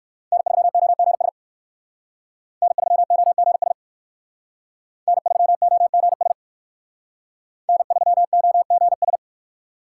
Morse code, heard as a keyed tone, about 53 wpm, 700 Hz. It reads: N3OGS N3OGS N3OGS N3OGS
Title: Speed Racing - USA 1x3 Call Signs 45wpm